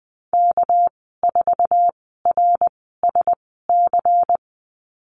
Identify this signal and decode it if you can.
Morse code, heard as a keyed tone, about 20 wpm, spaced as for 14 wpm, 700 Hz. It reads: K4RSC
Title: 231 Lesson 37g - Letter x - Call Signs 20wpm